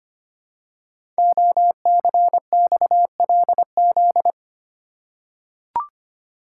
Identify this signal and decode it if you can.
Morse code, heard as a keyed tone, about 25 wpm, 700 Hz. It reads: OCXLZ
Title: Code Groups - 5 Letters - 25wpm